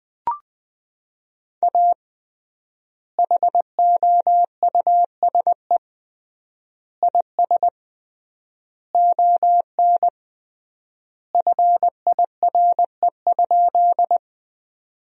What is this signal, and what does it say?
Morse code, heard as a keyed tone, about 21 wpm, 700 Hz. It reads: A HOUSE IS ON FIRE?
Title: 3x Word-Spacing - Sentences from Top 400 Words - 20wpm